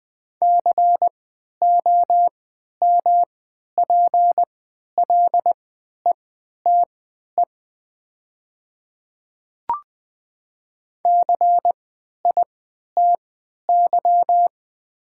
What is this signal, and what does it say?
Morse code, heard as a keyed tone, about 20 wpm, spaced as for 11 wpm, 700 Hz. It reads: COMPLETE CITY
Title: 3x Character Spacing - Top 500 Words - 20wpm